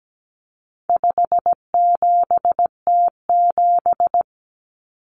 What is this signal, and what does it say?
Morse code, heard as a keyed tone, about 17 wpm, 700 Hz. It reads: 57T7
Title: Common Cut Numbers - 4-Digit 17wpm